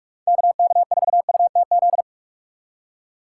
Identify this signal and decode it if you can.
Morse code, heard as a keyed tone, about 57 wpm, 700 Hz. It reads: KK4UTZ